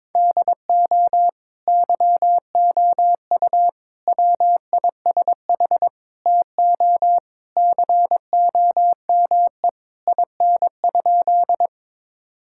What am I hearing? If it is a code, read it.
Morse code, heard as a keyed tone, about 21 wpm, 700 Hz. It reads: DO YOU WISH TO COME IN?
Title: Sentences from Top 1600 Words 22wpm